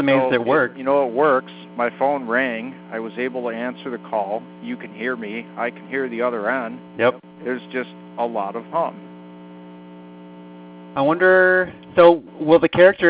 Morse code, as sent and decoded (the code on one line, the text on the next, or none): none